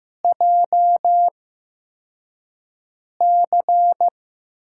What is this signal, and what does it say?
Morse code, heard as a keyed tone, about 15 wpm, spaced as for 4 wpm, 700 Hz. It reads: JC
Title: USA Call Signs - 8x Character Spacing - 15wpm